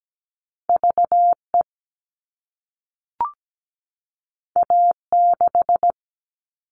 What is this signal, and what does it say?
Morse code, heard as a keyed tone, about 17 wpm, 700 Hz. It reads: VE A6
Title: All Cut Numbers - 2-Digit 17wpm